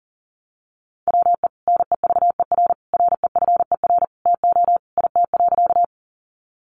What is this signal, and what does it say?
Morse code, heard as a keyed tone, about 40 wpm, 700 Hz. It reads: WE NEVER REFER TO IT.
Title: Sentences from Top 1000 Words 40wpm